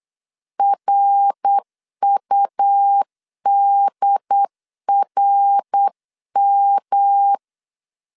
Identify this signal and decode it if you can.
Morse code, heard as a keyed tone, about 8 wpm, 800 Hz. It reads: RUDRM